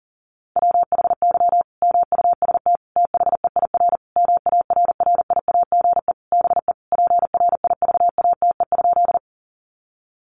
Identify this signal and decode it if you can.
Morse code, heard as a keyed tone, about 40 wpm, 700 Hz. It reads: WHY MUST THEIR MARRIAGE BE PRIVATE?
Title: Sentences from Top 1000 Words 40wpm